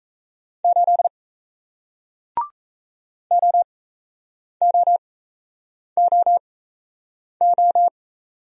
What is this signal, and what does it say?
Morse code, heard as a keyed tone, about 37 wpm, spaced as for 8 wpm, 700 Hz. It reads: 8 OOOO